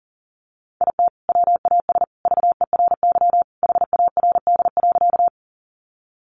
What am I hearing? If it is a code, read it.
Morse code, heard as a keyed tone, about 40 wpm, 700 Hz. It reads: IT WAS VERY HARD.
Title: Sentences from Top 300 Words 40wpm